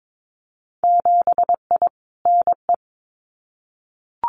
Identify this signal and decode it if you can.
Morse code, heard as a keyed tone, about 22 wpm, 700 Hz. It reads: 7I NE
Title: Winter Field Day Exchanges Encore 22wpm